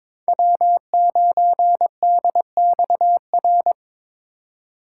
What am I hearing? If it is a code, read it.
Morse code, heard as a keyed tone, about 22 wpm, 700 Hz. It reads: W9DXR
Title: K1USN SST QSO Practice - 22wpm